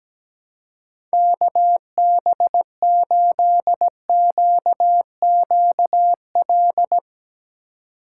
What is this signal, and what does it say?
Morse code, heard as a keyed tone, about 18 wpm, 700 Hz. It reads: KB8QQL
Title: Speed Racing - USA 2x3 Call Signs 17wpm